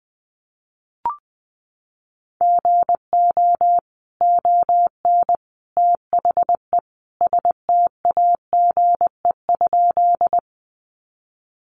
Morse code, heard as a keyed tone, about 19 wpm, 700 Hz. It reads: GO ON THE STAGE?
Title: Sentences from Top 1200 Words 20wpm